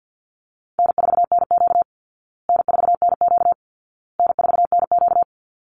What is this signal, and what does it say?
Morse code, heard as a keyed tone, about 48 wpm, 700 Hz. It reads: N4NQ N4NQ N4NQ
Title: Easy Call Signs Repeated 3x - 50wpm